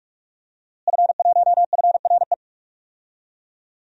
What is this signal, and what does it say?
Morse code, heard as a keyed tone, about 43 wpm, 700 Hz. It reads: F1FRE
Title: DXCC-100 Call Signs - 5 Characters - Repeated 3x - 45wpm